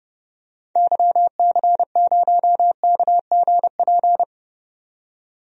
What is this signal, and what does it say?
Morse code, heard as a keyed tone, about 32 wpm, 700 Hz. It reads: YC0KGP
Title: DXCC-100 Call Signs - 6 Characters - 30wpm